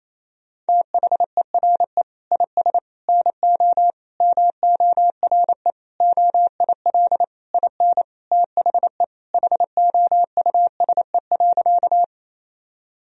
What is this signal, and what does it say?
Morse code, heard as a keyed tone, about 28 wpm, 700 Hz. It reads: THERE IS NO MORE OIL IN THE HOUSE.
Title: Sentences from Top 1000 Words - Review Part 4 - 28wpm